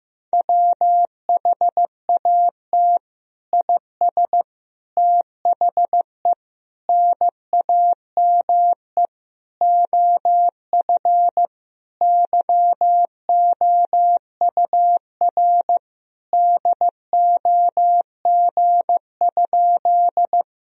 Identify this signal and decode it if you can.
Morse code, heard as a keyed tone, about 15 wpm, 700 Hz. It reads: WHAT IS THE NAME OF YOUR DOG?
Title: Sentences from Top 1000 Words - Review Part 2 - 15wpm